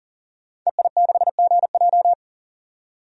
Morse code, heard as a keyed tone, about 40 wpm, 700 Hz. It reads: EI6GJ
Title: DXCC-100 Call Signs - 5 Characters - 40wpm